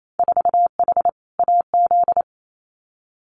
Morse code, heard as a keyed tone, about 28 wpm, 700 Hz. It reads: 4H AZ
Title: Winter Field Day Exchanges Encore 28wpm